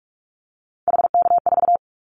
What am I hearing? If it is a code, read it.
Morse code, heard as a keyed tone, about 44 wpm, 700 Hz. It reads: HK4
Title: Callsign Prefixes for Top 100 DXCC - 45wpm